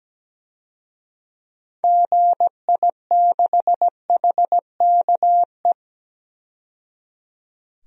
Morse code, heard as a keyed tone, about 17 wpm, 700 Hz. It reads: GI6HKE
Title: DXCC-100 Call Signs - 6 Characters - 17wpm